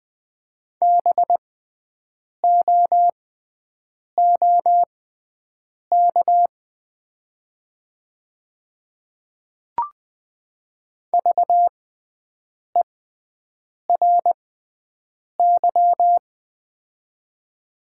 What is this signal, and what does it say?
Morse code, heard as a keyed tone, about 20 wpm, spaced as for 7 wpm, 700 Hz. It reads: BOOK VERY
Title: 6x Character Spacing - Top 300 Words - 20wpm